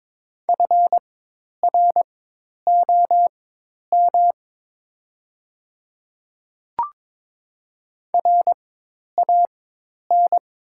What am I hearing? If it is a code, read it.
Morse code, heard as a keyed tone, about 22 wpm, spaced as for 10 wpm, 700 Hz. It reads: FROM RAN
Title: 4x Character Spacing - Top 500 Words - 22wpm